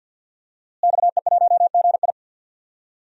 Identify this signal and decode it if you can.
Morse code, heard as a keyed tone, about 49 wpm, 700 Hz. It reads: XE1GI
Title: DXCC-100 Call Signs - 5 Characters - Repeated 2x - 50wpm